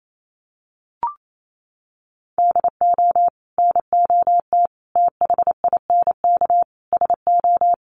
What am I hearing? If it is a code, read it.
Morse code, heard as a keyed tone, about 28 wpm, 700 Hz. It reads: DO NOT THINK SO
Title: Sets of 4 Words 28wpm